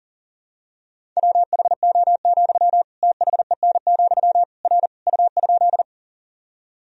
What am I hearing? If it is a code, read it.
Morse code, heard as a keyed tone, about 40 wpm, 700 Hz. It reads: WHO, THEN, R U?
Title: Sentences from Top 100 Words and Abbreviations 40wpm